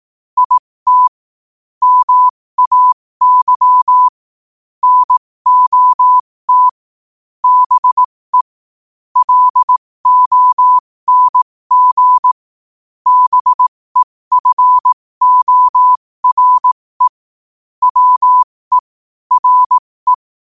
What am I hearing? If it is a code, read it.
Morse code, heard as a keyed tone, about 18 wpm, spaced as for 15 wpm, 1000 Hz. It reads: IT MAY NOT BE LONG BEFORE WE RE
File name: qotdfiles / 15wpm